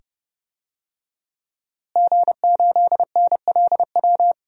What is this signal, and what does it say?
Morse code, heard as a keyed tone, about 30 wpm, 700 Hz. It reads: G8NLW
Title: DXCC-100 Call Signs - 5 Characters - Repeated 2x - 30wpm